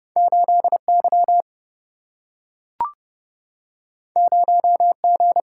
Morse code, heard as a keyed tone, about 29 wpm, 700 Hz. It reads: 8Y 0G
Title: Single-Letter Single-Number 30wpm